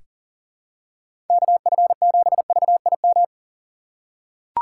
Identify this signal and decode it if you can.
Morse code, heard as a keyed tone, about 40 wpm, 700 Hz. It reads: KF7VIM
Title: USA Call Signs - 6 Characters - 6x Character Spacing - 40wpm